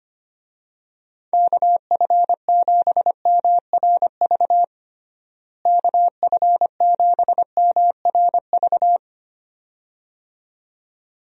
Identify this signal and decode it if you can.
Morse code, heard as a keyed tone, about 25 wpm, 700 Hz. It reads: KF7MRV KF7MRV
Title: USA Call Signs - 6 Characters - Repeated 2x - 25wpm